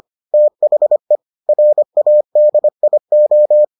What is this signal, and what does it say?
Morse code, heard as a keyed tone, about 25 wpm, 600 Hz. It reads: THE RADIO